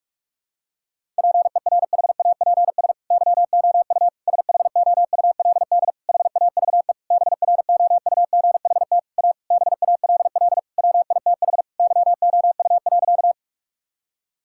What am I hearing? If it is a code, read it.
Morse code, heard as a keyed tone, about 46 wpm, 700 Hz. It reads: PERHAPS YOU SHOULD HAVE BROUGHT A BALL WITH YOU.
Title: Sentences from Top 1000 Words - Review Part 6 - 45wpm